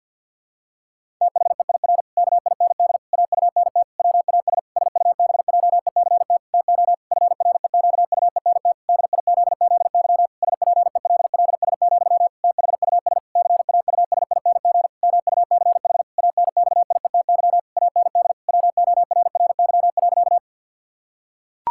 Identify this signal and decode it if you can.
Morse code, heard as a keyed tone, about 50 wpm, 700 Hz. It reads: THEIR KIND AUNT WAS SUBJECT TO FREQUENT DIZZY SPELLS, THUS CAUSING MUCH ANXIETY AND WORRY.
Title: Pangram Challenge v2 - 50wpm